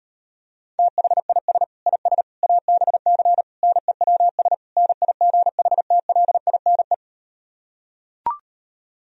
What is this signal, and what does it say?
Morse code, heard as a keyed tone, about 38 wpm, 700 Hz. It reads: THIS IS ABC NEWS NIGHTLINE